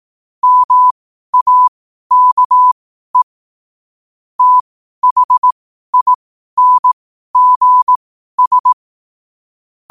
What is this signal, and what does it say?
Morse code, heard as a keyed tone, about 18 wpm, spaced as for 12 wpm, 1000 Hz. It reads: MAKE THINGS
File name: qotdfiles / 12wpm